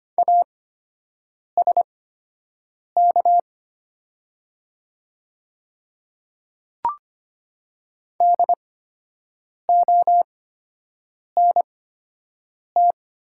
Morse code, heard as a keyed tone, about 25 wpm, spaced as for 7 wpm, 700 Hz. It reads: ASK DONT
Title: 8x Character Spacing - Top 300 Words - 25wpm